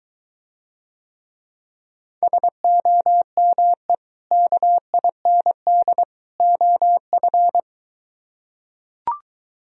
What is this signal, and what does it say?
Morse code, heard as a keyed tone, about 23 wpm, 700 Hz. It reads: SOME KIND OF